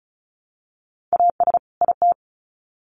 Morse code, heard as a keyed tone, about 34 wpm, 700 Hz. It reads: AS IT
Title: Sets of 2 Words 35wpm